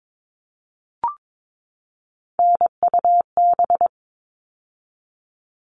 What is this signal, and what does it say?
Morse code, heard as a keyed tone, about 22 wpm, 700 Hz. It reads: NUB
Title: All Cut Numbers - 3-Digit 22wpm